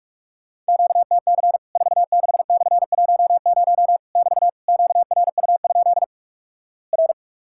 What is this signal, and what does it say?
Morse code, heard as a keyed tone, about 46 wpm, 700 Hz. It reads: QTC 46/10 <BT> QRU? R
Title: Worked All Europe DX Contest - QTC Exchange - 45wpm